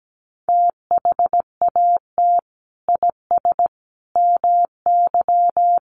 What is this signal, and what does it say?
Morse code, heard as a keyed tone, about 17 wpm, 700 Hz. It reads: THAT IS MY
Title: Sentences from Top 700 Words 17wpm